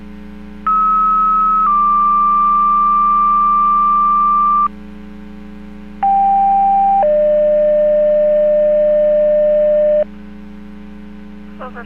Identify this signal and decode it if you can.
Two-tone page, heard as a one-second tone, then a three-second tone, about 1290 then 1180 Hz, about 790 then 600 Hz, with speech. It is heard from 0.7 to 10.0 s.